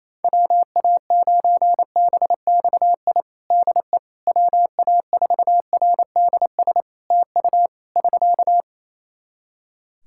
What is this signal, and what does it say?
Morse code, heard as a keyed tone, about 28 wpm, 700 Hz. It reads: WA9BXI DE WA4RDS TU <SK>
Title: Practice QSO - Exchange 4 - Ending - 28wpm